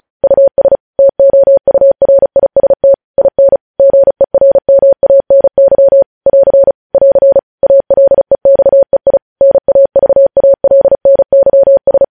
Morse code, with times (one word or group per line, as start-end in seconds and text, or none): 0.2-0.8 US
1.0-2.9 TOURIST
3.2-3.6 IN
3.8-6.0 GERMANY
6.3-6.7 <AR>
6.9-7.4 <AR>
7.6-9.2 ALEXEI
9.4-12.0 NAVALNYS